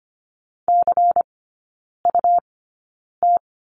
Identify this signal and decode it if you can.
Morse code, heard as a keyed tone, about 25 wpm, spaced as for 9 wpm, 700 Hz. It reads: CUT
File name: Lesson 17e - Letter u - Three Characters 25wpm